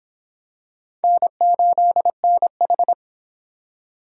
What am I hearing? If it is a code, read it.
Morse code, heard as a keyed tone, about 25 wpm, 700 Hz. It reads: N8NH